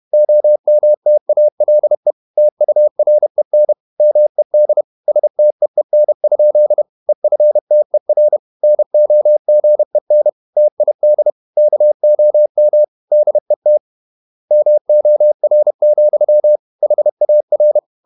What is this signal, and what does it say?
Morse code, heard as a keyed tone, about 31 wpm, 600 Hz. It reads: OMTALE TUREN MED STEEN? EFTER NOGEN TID KOM DET MOR, HAR
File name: Ferie QRQ 31wpm